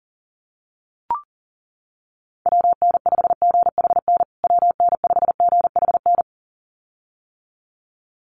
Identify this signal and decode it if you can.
Morse code, heard as a keyed tone, about 37 wpm, 700 Hz. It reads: WN5GHN WN5GHN
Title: USA Call Signs Repeated 2x 40wpm